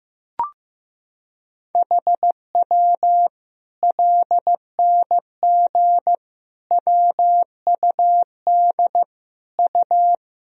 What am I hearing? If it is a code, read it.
Morse code, heard as a keyed tone, about 16 wpm, 700 Hz. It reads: HW LNG WUD U
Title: Sentences from Top 500 Words and Abbreviations 15wpm